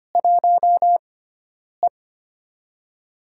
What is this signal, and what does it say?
Morse code, heard as a keyed tone, about 25 wpm, spaced as for 9 wpm, 700 Hz. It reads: 1E
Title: USA Call Signs - 6x Character Spacing - 25wpm